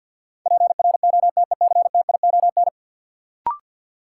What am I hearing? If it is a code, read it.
Morse code, heard as a keyed tone, about 46 wpm, 700 Hz. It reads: PROTECTION
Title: Sentences from Top 1600 Words - New Word Drill Alt - 50wpm